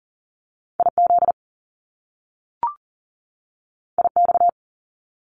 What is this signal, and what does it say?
Morse code, heard as a keyed tone, about 42 wpm, 700 Hz. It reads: IZ IX
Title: Top Two Letter Combinations in Words 40wpm